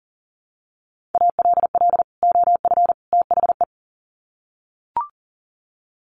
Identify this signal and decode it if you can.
Morse code, heard as a keyed tone, about 40 wpm, 700 Hz. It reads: ALL OF THE
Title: Sets of 3 Words 40wpm